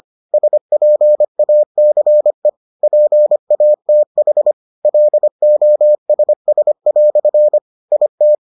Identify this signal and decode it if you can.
Morse code, heard as a keyed tone, about 25 wpm, 600 Hz. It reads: SPACE PATH LOSS" IT